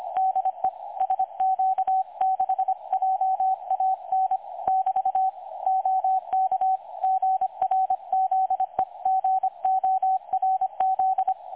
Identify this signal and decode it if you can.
Morse code, heard as a keyed tone, about 25 wpm, 760 Hz, with radio noise. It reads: DE SQ6JAN <BT> OK GRZEGORZ
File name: dlazaawansowanych / nagraniazpasma / websdr recording 3565.0kHz